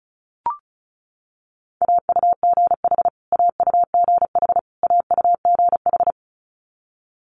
Morse code, heard as a keyed tone, about 36 wpm, 700 Hz. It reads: AUGH AUGH AUGH
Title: Multi-letter Phonemes Repeated 3x 35wpm